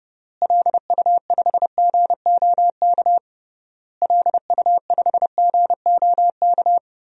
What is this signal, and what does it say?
Morse code, heard as a keyed tone, about 32 wpm, 700 Hz. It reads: LU5GOK LU5GOK
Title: DXCC-100 Call Signs - 6 Characters - Repeated 3x - 30wpm